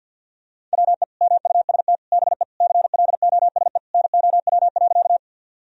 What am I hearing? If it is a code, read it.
Morse code, heard as a keyed tone, about 51 wpm, 700 Hz. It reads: WE MUST BE CLOSE NOW.
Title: Sentences from Top 1000 Words - Review Part 1 - 50wpm